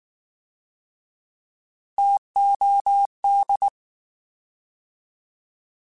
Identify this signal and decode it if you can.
Morse code, heard as a keyed tone, about 19 wpm, 800 Hz. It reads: TOD